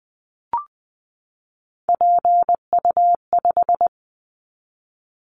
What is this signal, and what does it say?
Morse code, heard as a keyed tone, about 20 wpm, 700 Hz. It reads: PU5
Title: Callsign Prefixes for Top 50 DXCC - 20wpm